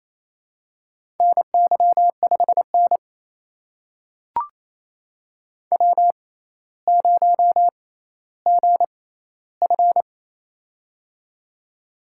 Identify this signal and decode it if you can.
Morse code, heard as a keyed tone, about 29 wpm, spaced as for 15 wpm, 700 Hz. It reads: NY5N W0GF
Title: USA Call Signs - 4 Characters - 6x Character Spacing - 28wpm